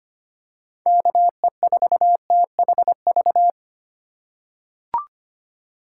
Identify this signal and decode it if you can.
Morse code, heard as a keyed tone, about 25 wpm, 700 Hz. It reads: KE4THV
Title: USA Call Signs - 6 Characters - 3x Character Spacing - 25wpm